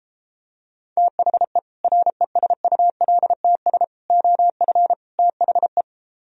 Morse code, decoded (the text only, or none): THE RESULTS OF THE